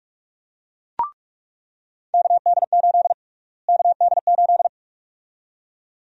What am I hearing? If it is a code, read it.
Morse code, heard as a keyed tone, about 42 wpm, 700 Hz. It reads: KD8 KD8
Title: USA Call Sign Prefixes - Repeated 2x - 45wpm